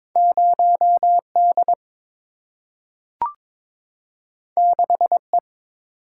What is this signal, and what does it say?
Morse code, heard as a keyed tone, about 22 wpm, 700 Hz. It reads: 0D 6E
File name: practice / 22wpm / Two Letter-Number Combinations - 22wpm